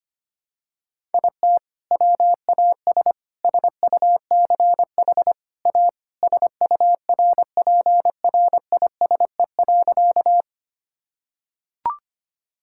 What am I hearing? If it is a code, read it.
Morse code, heard as a keyed tone, about 25 wpm, 700 Hz. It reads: IT WAS SUCH A SURPRISE.
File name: practice / 25wpm / Sentences with 2-Word Phrases - 25wpm